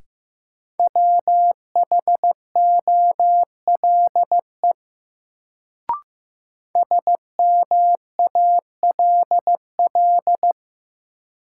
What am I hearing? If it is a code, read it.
Morse code, heard as a keyed tone, about 15 wpm, 700 Hz. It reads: WHOLE SMALL
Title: Sets of 2 Words - New Word Drill - 15wpm